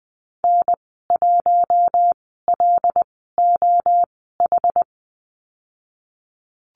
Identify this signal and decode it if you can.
Morse code, heard as a keyed tone, about 20 wpm, spaced as for 14 wpm, 700 Hz. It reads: N1LOH
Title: 088 Lesson 15g - Number 5 - Call Signs 20wpm